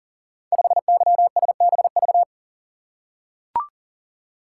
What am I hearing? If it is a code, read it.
Morse code, heard as a keyed tone, about 42 wpm, 700 Hz. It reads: 5YSBV
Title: Code Groups - 5 Letter:Numbers - 40wpm